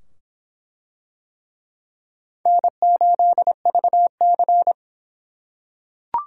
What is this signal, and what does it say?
Morse code, heard as a keyed tone, about 25 wpm, 700 Hz. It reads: N8VC